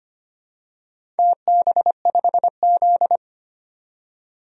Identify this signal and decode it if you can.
Morse code, heard as a keyed tone, about 25 wpm, 700 Hz. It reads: TB5Z
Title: DXCC-100 Call Signs - 4 Characters - 25wpm